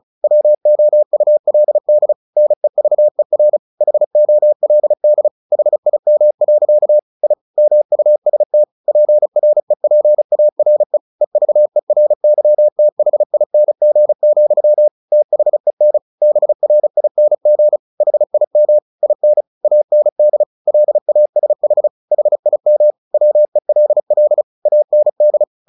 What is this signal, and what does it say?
Morse code, heard as a keyed tone, about 34 wpm, 600 Hz. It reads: WOULD NEVER HOLD HIM. I MUST PREPARE EVERYTHING, THEN BRING HIM IN AND LASH HIM WELL AND
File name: the old man and the sea 2 35wpm